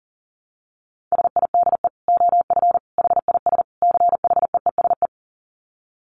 Morse code, heard as a keyed tone, about 40 wpm, 700 Hz. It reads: SIDE OF HIS CHEESE
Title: Dit Challenge 40wpm